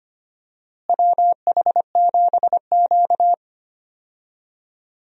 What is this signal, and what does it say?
Morse code, heard as a keyed tone, about 25 wpm, 700 Hz. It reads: WH7Q
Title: USA Call Signs - 4 Characters - Repeated 3x - 25wpm